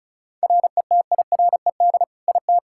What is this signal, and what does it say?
Morse code, heard as a keyed tone, about 36 wpm, 700 Hz. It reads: RETIRED IT
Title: Practice QSO - Exchange 3 - More Info - 35wpm